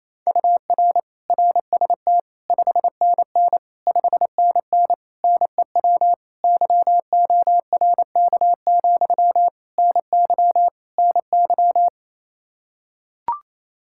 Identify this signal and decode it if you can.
Morse code, heard as a keyed tone, about 28 wpm, 700 Hz. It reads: UR RST 5NN 5NN NEW YORK, NY NY
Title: Practice QSO - Exchange 1 - Beginning - 28wpm